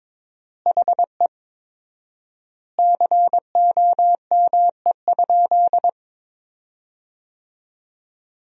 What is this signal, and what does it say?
Morse code, heard as a keyed tone, about 22 wpm, 700 Hz. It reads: HE COME?
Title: 4x Word-Spacing - Sentences from Top 200 Words - 22wpm